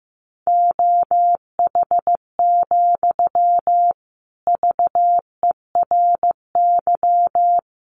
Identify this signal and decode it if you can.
Morse code, heard as a keyed tone, about 15 wpm, 700 Hz. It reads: OH, VERY
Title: Sentences from Top 1000 Words 15wpm